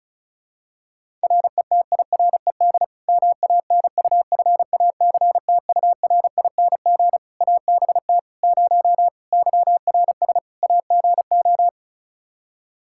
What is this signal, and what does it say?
Morse code, heard as a keyed tone, about 34 wpm, 700 Hz. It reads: RETIRED MANUFACTURING ABT 0 YRS AGO
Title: Practice QSO - Exchange 3 - More Info - 35wpm